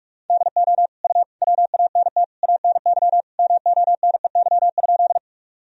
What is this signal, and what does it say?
Morse code, heard as a keyed tone, about 46 wpm, 700 Hz. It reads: DO U WANT ANY MONEY?
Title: Sentences from Top 400 Words and Abbreviations 45wpm